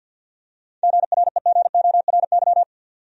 Morse code, heard as a keyed tone, about 49 wpm, 700 Hz. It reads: GREGORY
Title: USA Common Names - 50wpm